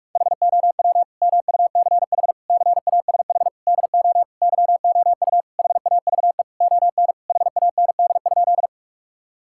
Morse code, heard as a keyed tone, about 41 wpm, 690 Hz. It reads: HOW MUCH CASH DO YOU HAVE ON HAND?
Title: Sentences from Top 1000 Words - Review Part 5 - 45wpm